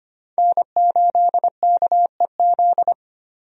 Morse code, heard as a keyed tone, about 25 wpm, 700 Hz. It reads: N8KEZ
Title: USA Call Signs - 5 Characters - Repeated 2x - 25wpm